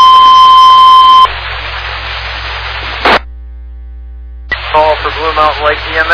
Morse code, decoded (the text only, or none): none